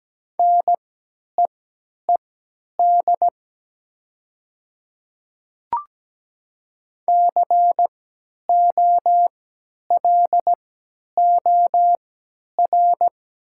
Morse code, heard as a keyed tone, about 17 wpm, spaced as for 10 wpm, 690 Hz. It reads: NEED COLOR
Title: 3x Character Spacing - Top 300 Words - 17wpm